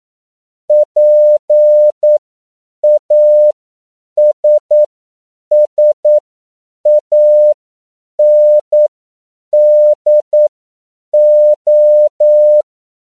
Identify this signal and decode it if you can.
Morse code, heard as a keyed tone, about 9 wpm, spaced as for 7 wpm, 600 Hz. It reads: PASSANDO